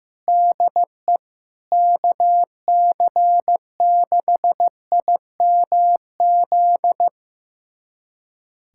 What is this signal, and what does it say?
Morse code, heard as a keyed tone, about 15 wpm, 700 Hz. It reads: DE KC6IMZ
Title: USA Callsign DE Callsign 15wpm